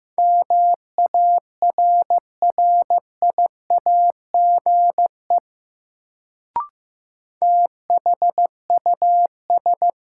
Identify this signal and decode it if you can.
Morse code, heard as a keyed tone, about 15 wpm, 700 Hz. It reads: MARRIAGE THUS
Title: Sentences from Top 800 Words - New Word Drill - 15wpm